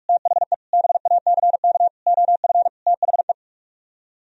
Morse code, heard as a keyed tone, about 49 wpm, 700 Hz. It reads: THE BACK OF THE